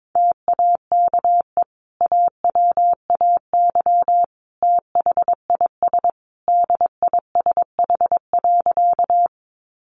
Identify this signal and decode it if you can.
Morse code, heard as a keyed tone, about 22 wpm, 700 Hz. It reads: TAKE AWAY THIS DISH.
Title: Sentences from Top 1300 Words 22wpm